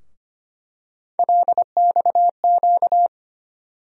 Morse code, heard as a keyed tone, about 25 wpm, 700 Hz. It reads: LXQ
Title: USA Call Sign Postfixes - 3 Letters 25wpm